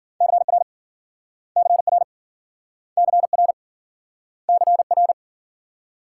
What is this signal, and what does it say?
Morse code, heard as a keyed tone, about 54 wpm, 700 Hz. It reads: CR CR CR CR